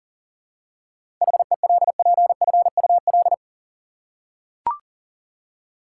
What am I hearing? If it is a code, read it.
Morse code, heard as a keyed tone, about 37 wpm, 700 Hz. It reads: HELPFUL
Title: Seven Letter Words - 40wpm